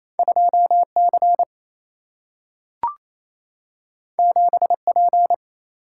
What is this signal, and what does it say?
Morse code, heard as a keyed tone, about 28 wpm, 700 Hz. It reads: 2 C 7 P
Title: Single-Letter Single-Number 28wpm